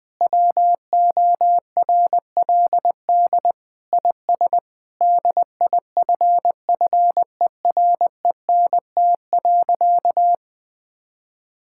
Morse code, heard as a keyed tone, about 20 wpm, 700 Hz. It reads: WORLD IS DIFFERENT.
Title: Sentences with 4-Word Phrases - Spoken Beginning - 20wpm